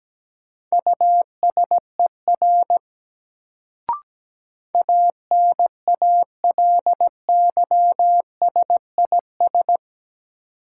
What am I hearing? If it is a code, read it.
Morse code, heard as a keyed tone, about 17 wpm, 700 Hz. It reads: USER ANALYSIS
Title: Sentences from Top 1000 Words - Review Part 5 - New Word Drill - 17wpm